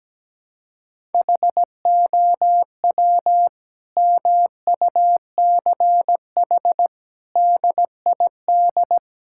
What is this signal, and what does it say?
Morse code, heard as a keyed tone, about 17 wpm, 700 Hz. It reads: HOW MUCH DID
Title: Sentences from Top 1000 Words - Review Part 5 - 17wpm